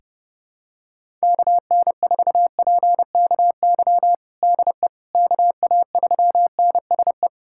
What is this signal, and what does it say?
Morse code, heard as a keyed tone, about 32 wpm, 700 Hz. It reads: KN4PKY DE KA3NSE
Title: USA Callsign DE Callsign - 6 Characters - 30wpm